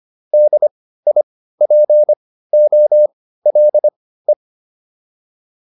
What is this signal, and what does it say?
Morse code, heard as a keyed tone, about 25 wpm, spaced as for 15 wpm, 600 Hz. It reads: DIPOLE
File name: QSO 405 15